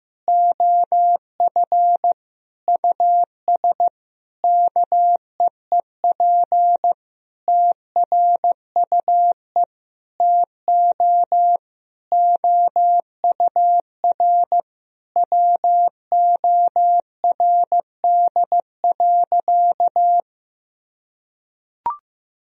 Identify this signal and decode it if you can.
Morse code, heard as a keyed tone, about 14 wpm, 700 Hz. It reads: OF US KEEP TRUE TO OUR WORD.
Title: Sentences from Top 1000 Words - Review Part 3 - 15wpm